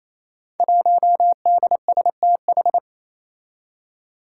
Morse code, heard as a keyed tone, about 27 wpm, 700 Hz. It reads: 1DSTH
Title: Five Challenging Letter-Number Combinations 28wpm